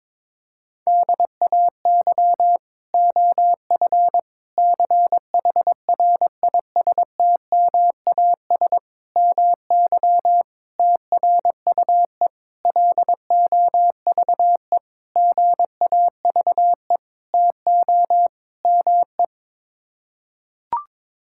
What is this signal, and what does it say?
Morse code, heard as a keyed tone, about 22 wpm, 700 Hz. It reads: DAY OF CHRISTMAS MY TRUE LOVE GAVE TO ME
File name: Sets of 9 Words - 22wpm